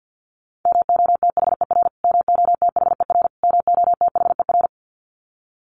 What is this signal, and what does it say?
Morse code, heard as a keyed tone, about 50 wpm, 700 Hz. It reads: MOTHER MOTHER MOTHER
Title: Top 300 Words Repeated 50wpm